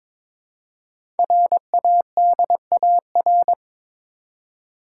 Morse code, heard as a keyed tone, about 22 wpm, 690 Hz. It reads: RADAR
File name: 5 Letter Words - First 3 Letters Given - 22wpm